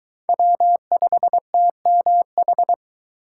Morse code, heard as a keyed tone, about 23 wpm, 700 Hz. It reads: W5TMH